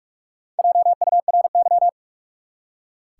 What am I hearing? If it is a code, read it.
Morse code, heard as a keyed tone, about 42 wpm, 700 Hz. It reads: JURY
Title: Four Letter Words - 45wpm